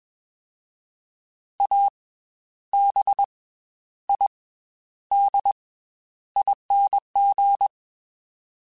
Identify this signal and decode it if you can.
Morse code, heard as a keyed tone, about 21 wpm, spaced as for 11 wpm, 800 Hz. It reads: ABIDING